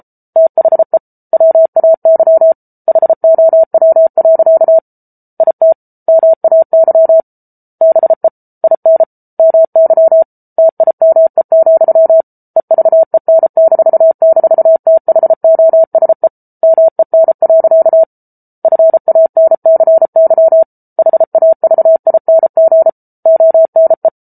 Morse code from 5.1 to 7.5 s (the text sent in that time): IT MAY